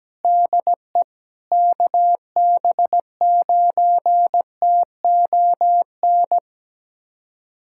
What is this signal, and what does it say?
Morse code, heard as a keyed tone, about 17 wpm, 700 Hz. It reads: DE KB9TON
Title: USA Callsign DE Callsign 17wpm